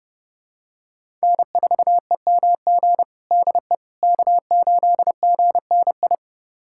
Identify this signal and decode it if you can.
Morse code, heard as a keyed tone, about 30 wpm, 700 Hz. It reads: N4EMG DE K8GNI
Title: USA Callsign DE Callsign - 5 Characters - 30wpm